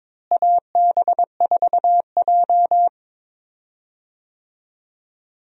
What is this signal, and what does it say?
Morse code, heard as a keyed tone, about 22 wpm, 700 Hz. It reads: AB4J
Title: USA Call Signs - 4 Characters - Repeated 2x - 22wpm